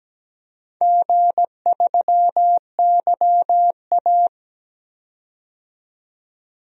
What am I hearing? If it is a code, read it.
Morse code, heard as a keyed tone, about 17 wpm, 700 Hz. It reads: G3YA
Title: DXCC-100 Call Signs - 4 Characters - 17wpm